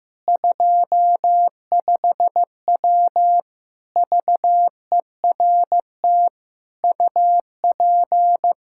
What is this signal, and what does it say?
Morse code, heard as a keyed tone, about 15 wpm, 700 Hz. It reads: 25W VERT UP
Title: Practice QSO - Exchange 2 - Info - 15wpm